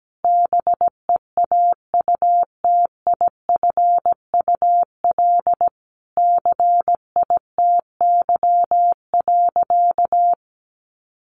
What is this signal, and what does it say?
Morse code, heard as a keyed tone, about 17 wpm, 700 Hz. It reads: BEAUTIFUL CITY.
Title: Sentences from Top 1100 Words 17wpm